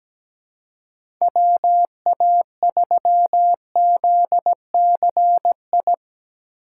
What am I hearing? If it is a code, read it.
Morse code, heard as a keyed tone, about 17 wpm, 700 Hz. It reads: WA3ZCI
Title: USA Call Signs - 6x Character Spacing - 17wpm